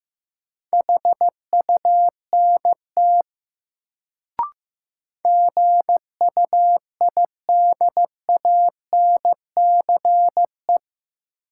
Morse code, heard as a keyed tone, about 15 wpm, 700 Hz. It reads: HUNT GUIDANCE
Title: Sentences from Top 1600 Words - New Word Drill Alt - 15wpm